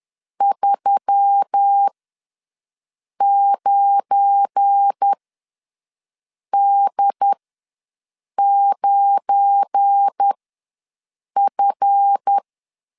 Morse code, heard as a keyed tone, about 10 wpm, spaced as for 5 wpm, 800 Hz. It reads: 39D9F
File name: LEZIONI / 7 GF9Z / 7 GF9 ws 10wpm 4034